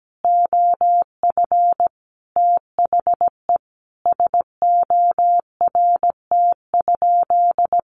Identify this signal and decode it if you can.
Morse code, heard as a keyed tone, about 17 wpm, 700 Hz. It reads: OF THE SORT?
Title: Sentences from Top 1300 Words 17wpm